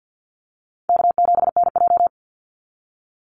Morse code, heard as a keyed tone, about 48 wpm, 700 Hz. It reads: K7NJ
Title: Call Signs Repeated 3x - 50wpm